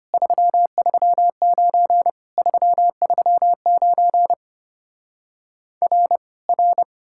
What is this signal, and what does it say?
Morse code, heard as a keyed tone, about 33 wpm, 700 Hz. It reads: 339 339 R R